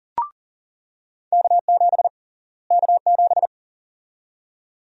Morse code, heard as a keyed tone, about 40 wpm, 700 Hz. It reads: K7 K7
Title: USA Call Sign Prefixes - Repeated 2x - 40wpm